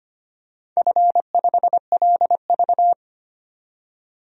Morse code, heard as a keyed tone, about 25 wpm, 700 Hz. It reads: F5LV
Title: DXCC-100 Call Signs - 4 Characters - 25wpm